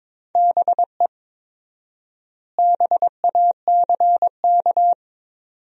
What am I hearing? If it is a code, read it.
Morse code, heard as a keyed tone, about 22 wpm, 700 Hz. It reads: BE BACK
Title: 4x Word-Spacing - Sentences from Top 500 Words - 22wpm